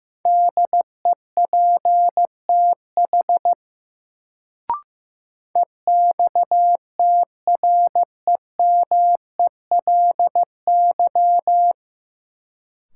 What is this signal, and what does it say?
Morse code, heard as a keyed tone, about 16 wpm, 700 Hz. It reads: DEPTH EXTREMELY
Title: Sentences from Top 1200 Words - New Word Drill Alt - 15wpm